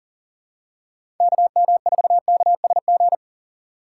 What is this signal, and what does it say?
Morse code, heard as a keyed tone, about 40 wpm, 700 Hz. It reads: KM4KSG
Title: USA Call Signs - 2x Character Spacing - 40wpm